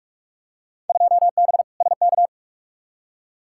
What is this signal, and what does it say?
Morse code, heard as a keyed tone, about 46 wpm, 700 Hz. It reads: 2B SK
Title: Field Day Exchanges Encore - 45wpm